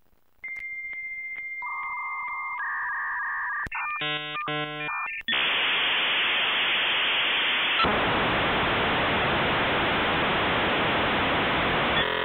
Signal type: dial-up modem handshake